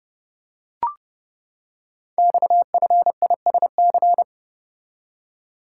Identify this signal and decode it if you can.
Morse code, heard as a keyed tone, about 28 wpm, 690 Hz. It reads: XFISC
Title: Code Groups - 5 Letters - 30wpm